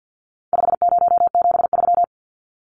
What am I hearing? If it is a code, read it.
Morse code, heard as a keyed tone, about 50 wpm, 700 Hz. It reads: 5073
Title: Numbers - 4-Digit 50wpm